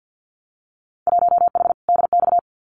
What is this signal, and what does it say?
Morse code, heard as a keyed tone, about 50 wpm, 700 Hz. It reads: 1H DX
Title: Winter Field Day Exchanges 50wpm